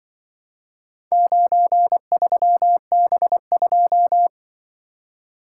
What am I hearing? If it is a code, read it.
Morse code, heard as a keyed tone, about 25 wpm, 700 Hz. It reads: 93B2
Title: Speed-Racing - Four Letter-Number Combinations - 20wpm